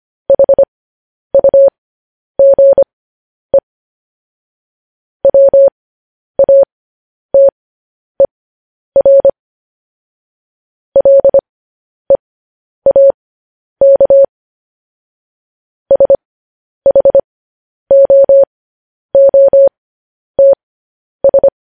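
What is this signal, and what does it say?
Morse code, heard as a keyed tone, about 25 wpm, spaced as for 10 wpm, 550 Hz. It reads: HUGE WATER LEAK SHOOTS
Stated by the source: Morse code news headlines sent at 10 wpm.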